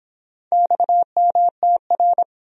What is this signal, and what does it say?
Morse code, heard as a keyed tone, about 24 wpm, 700 Hz. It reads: XMTR